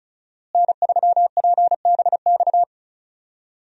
Morse code, heard as a keyed tone, about 36 wpm, 700 Hz. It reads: N3PBX
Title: USA Call Signs - 5 Characters - Repeated 2x - 35wpm